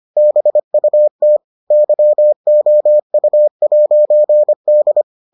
Morse code, heard as a keyed tone, about 25 wpm, 600 Hz. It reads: BUT YOU'D